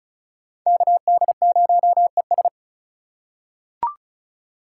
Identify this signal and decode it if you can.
Morse code, heard as a keyed tone, about 36 wpm, 700 Hz. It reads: KD0ES
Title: USA Call Signs - 5 Characters - 3x Character Spacing - 35wpm